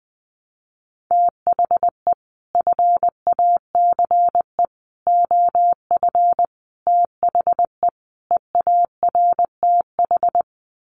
Morse code, heard as a keyed tone, about 20 wpm, 700 Hz. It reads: THE FACE OF THE EARTH
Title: Sets of 5 Words Encore 20wpm